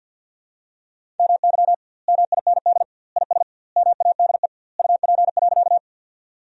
Morse code, heard as a keyed tone, about 46 wpm, 700 Hz. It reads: MY MIND IS MADE UP.
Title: 2x Word-Spacing - Sentences from Top 500 Words - 50wpm